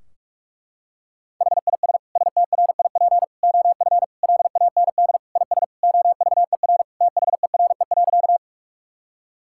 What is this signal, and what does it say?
Morse code, heard as a keyed tone, about 46 wpm, 700 Hz. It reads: HIS STRIP OF LAND IS OVER THERE.
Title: Sentences from Top 1600 Words 45wpm